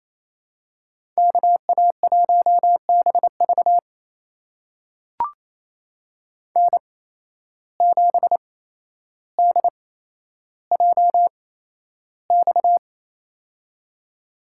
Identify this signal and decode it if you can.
Morse code, heard as a keyed tone, about 28 wpm, spaced as for 12 wpm, 700 Hz. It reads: KA1BV N7DJX
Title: USA Call Signs - 5 Characters - 8x Character Spacing - 28wpm